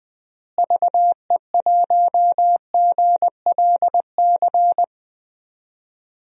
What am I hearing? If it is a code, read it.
Morse code, heard as a keyed tone, about 21 wpm, 700 Hz. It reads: VE1GLC
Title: DXCC-100 Call Signs - 6 Characters - Repeated 2x - 20wpm